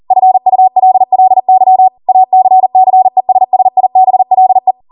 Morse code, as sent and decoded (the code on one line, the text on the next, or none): ..-. ..- .-.. .-.. -.-- / .- -.-. -.-. ... ... .. -... .-.. .
FULLY ACCESSIBLE